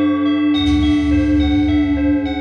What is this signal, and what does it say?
There is no code.